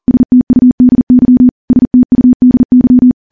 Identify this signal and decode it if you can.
Morse code, heard as a keyed tone, about 40 wpm, 250 Hz. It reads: STUDY STUDY
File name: morse / morse-training-40WPM